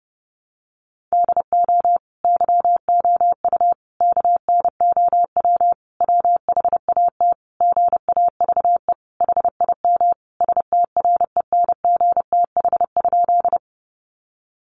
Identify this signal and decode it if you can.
Morse code, heard as a keyed tone, about 30 wpm, 700 Hz. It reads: DO YOU KNOW WHAT GAVE HIM STRENGTH?
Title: Sentences from Top 1300 Words 30wpm